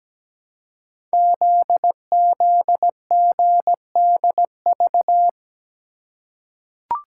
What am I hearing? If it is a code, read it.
Morse code, heard as a keyed tone, about 17 wpm, 700 Hz. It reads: ZZGDV
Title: Code Groups - 5 Letters - 17wpm